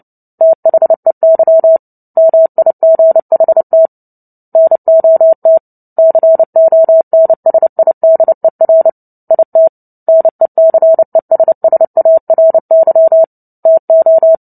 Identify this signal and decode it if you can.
Morse code, heard as a keyed tone, about 30 wpm, 650 Hz. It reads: THEY MIGHT NOT CONSIDER IT NECESSARY TO
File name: Book Two / 03-Chapter 3 - 29 WPM